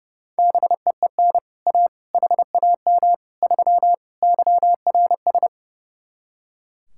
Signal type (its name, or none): Morse code